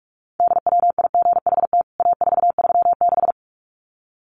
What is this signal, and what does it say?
Morse code, heard as a keyed tone, about 44 wpm, 700 Hz. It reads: DWIGHT A436
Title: CWT Exchange 45wpm